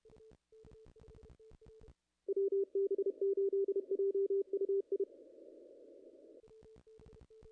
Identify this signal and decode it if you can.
Morse code, heard as a keyed tone, about 30 wpm, 410 Hz. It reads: WB8JUI
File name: contests / ss-cw-k5tr-op-n5rz -0000